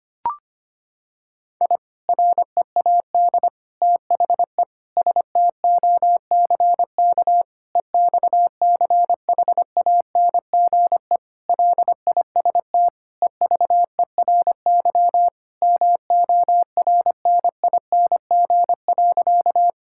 Morse code, heard as a keyed tone, about 25 wpm, 700 Hz. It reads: I READ THE STOCK EXCHANGE LIST EVERY MORNING.
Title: Sentences from Top 1000 Words - Review Part 6 - 25wpm